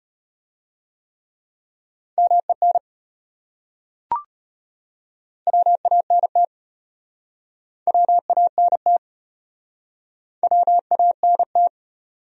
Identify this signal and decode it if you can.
Morse code, heard as a keyed tone, about 32 wpm, 700 Hz. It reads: MEN WANT WANT WANT